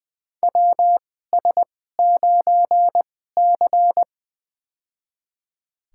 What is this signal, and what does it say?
Morse code, heard as a keyed tone, about 20 wpm, spaced as for 14 wpm, 700 Hz. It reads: WS9C
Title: USA Call Signs - 2x Character Spacing - 20wpm